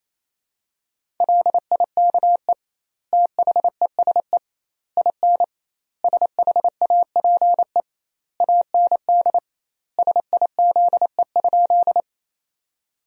Morse code, heard as a keyed tone, about 28 wpm, 700 Hz. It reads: LIKE THESE IN SHAPE AND SIZE?
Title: 2x Word-Spacing - Sentences from Top 500 Words - 28wpm